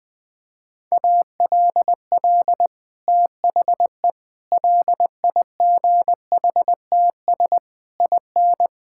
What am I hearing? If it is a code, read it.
Morse code, heard as a keyed tone, about 21 wpm, 700 Hz. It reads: ALL THE LIGHTS IN
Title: Sentences with 2-Word Phrases - Spoken Beginning - 20wpm